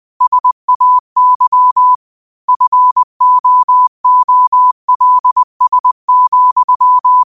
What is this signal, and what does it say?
Morse code, heard as a keyed tone, about 20 wpm, 1000 Hz. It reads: SAY FOOLS,
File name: qotdfiles / 20wpm